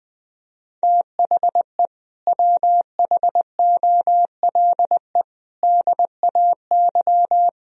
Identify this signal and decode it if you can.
Morse code, heard as a keyed tone, about 20 wpm, 700 Hz. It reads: THE WHOLE DAY
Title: Sentences from Top 1000 Words - Review Part 3 - 20wpm